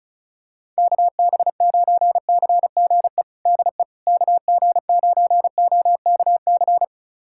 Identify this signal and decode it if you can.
Morse code, heard as a keyed tone, about 34 wpm, 700 Hz. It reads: KB9CGE DE KG9OKC
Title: USA Callsign DE Callsign - 6 Characters - 35wpm